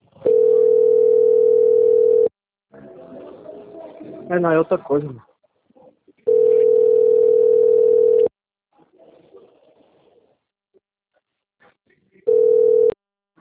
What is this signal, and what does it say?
Ringback tone (North America), 440 + 480 Hz, with speech.
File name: out- - - - - .8908